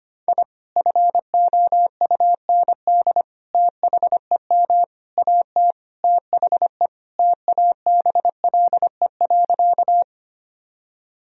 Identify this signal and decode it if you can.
Morse code, heard as a keyed tone, about 25 wpm, 700 Hz. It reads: I FOUND THEM AT THE TABLE.
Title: Sentences from Top 1000 Words - Review Part 2 - 25wpm